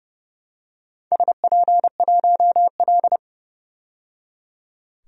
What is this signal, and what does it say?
Morse code, heard as a keyed tone, about 30 wpm, 700 Hz. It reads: SP1L
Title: DXCC-100 Call Signs - 4 Characters - 30wpm